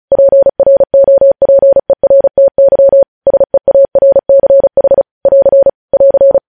Morse code, heard as a keyed tone, about 36 wpm, 550 Hz. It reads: PROPERTY SEARCH <AR> <AR>